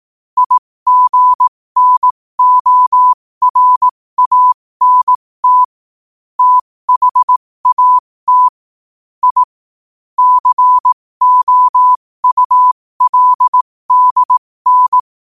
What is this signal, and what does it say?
Morse code, heard as a keyed tone, about 18 wpm, spaced as for 15 wpm, 1000 Hz. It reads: IGNORANT THAT I COULDN
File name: qotdfiles / 15wpm